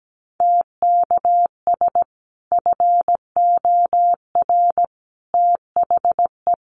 Morse code, heard as a keyed tone, about 17 wpm, 700 Hz. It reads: TKS FOR THE
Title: Common QSO Phrases 17wpm